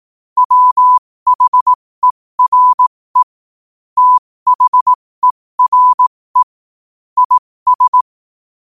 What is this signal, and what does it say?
Morse code, heard as a keyed tone, about 18 wpm, spaced as for 15 wpm, 1000 Hz. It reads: WHERE THERE IS
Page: New quotes every day in morse code at 15 Words per minute.